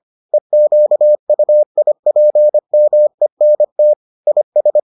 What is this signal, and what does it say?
Morse code, heard as a keyed tone, about 25 wpm, 600 Hz. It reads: EQUIPMENT IS